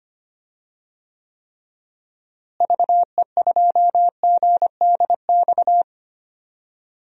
Morse code, heard as a keyed tone, about 25 wpm, 700 Hz. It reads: VE2GDX
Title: DXCC-100 Call Signs - 6 Characters - Repeated 3x - 25wpm